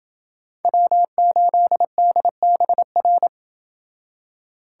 Morse code, heard as a keyed tone, about 26 wpm, 700 Hz. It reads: W8DBR